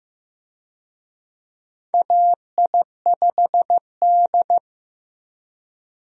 Morse code, heard as a keyed tone, about 15 wpm, 700 Hz. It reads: AI5D
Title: USA Call Signs - 4 Characters - Repeated 3x - 15wpm